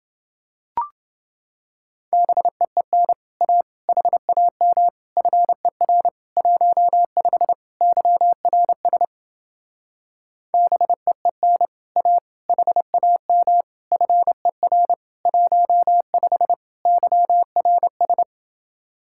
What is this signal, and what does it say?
Morse code, heard as a keyed tone, about 29 wpm, 700 Hz. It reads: BEEN A HAM FER 15 YRS BEEN A HAM FER 15 YRS